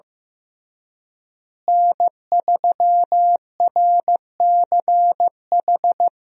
Morse code, heard as a keyed tone, about 16 wpm, 700 Hz. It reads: N3RCH
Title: USA Call Signs - 5 Characters - Repeated 3x - 15wpm